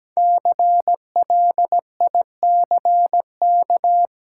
Morse code, heard as a keyed tone, about 17 wpm, 700 Hz. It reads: CLICK
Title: Sentences from Top 1100 Words - New Word Drill - 17wpm